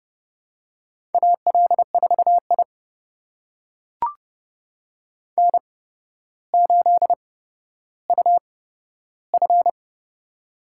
Morse code, heard as a keyed tone, about 30 wpm, spaced as for 13 wpm, 700 Hz. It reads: AL4I N8UF
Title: USA Call Signs - 4 Characters - 8x Character Spacing - 30wpm